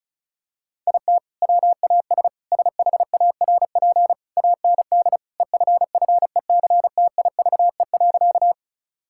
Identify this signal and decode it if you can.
Morse code, heard as a keyed tone, about 34 wpm, 700 Hz. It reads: IT WAS SHARP AND EFFECTIVE.
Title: Sentences from Top 1600 Words 35wpm